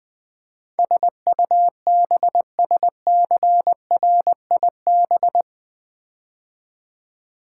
Morse code, heard as a keyed tone, about 20 wpm, 700 Hz. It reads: SUBSCRIB
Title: Build A Word - 4 Letters At A Time - 10-Letter Words - 20wpm